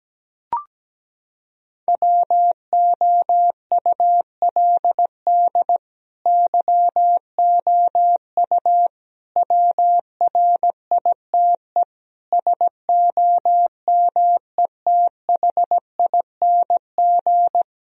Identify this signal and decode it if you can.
Morse code, heard as a keyed tone, about 17 wpm, 700 Hz. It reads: WOULD YOU WRITE SOMETHING
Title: Sentences from Top 1000 Words - Review Part 2 - 17wpm